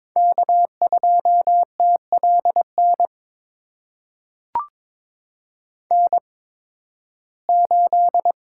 Morse code, heard as a keyed tone, about 22 wpm, spaced as for 14 wpm, 700 Hz. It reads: K2TLN N8
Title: USA Call Signs - 5 Characters - 8x Character Spacing - 22wpm